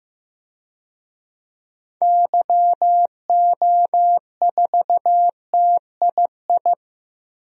Morse code, heard as a keyed tone, about 16 wpm, 700 Hz. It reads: YO4TII
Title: DXCC-100 Call Signs - 6 Characters - 15wpm